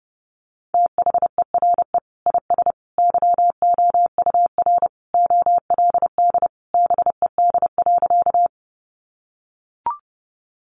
Morse code, heard as a keyed tone, about 28 wpm, 700 Hz. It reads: THERE IS YOUR OLD BED.
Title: Sentences from Top 500 Words 30wpm